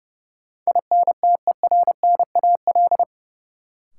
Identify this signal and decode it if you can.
Morse code, heard as a keyed tone, about 30 wpm, 700 Hz. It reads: INTERNAL
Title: Sentences from Top 1400 Words - New Word Drill - 30wpm